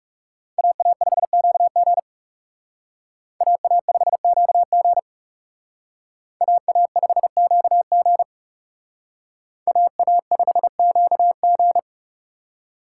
Morse code, heard as a keyed tone, about 39 wpm, 700 Hz. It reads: AA5QG AA5QG AA5QG AA5QG